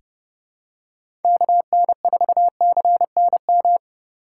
Morse code, heard as a keyed tone, about 30 wpm, 700 Hz. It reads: KN4CNM
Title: USA Call Signs - 6 Characters - 4x Character Spacing - 30wpm